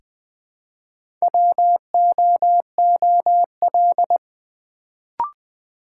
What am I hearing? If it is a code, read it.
Morse code, heard as a keyed tone, about 20 wpm, 700 Hz. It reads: WOOL
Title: Build A Word - 2 Letters At A Time - 4-Letter Words - 20wpm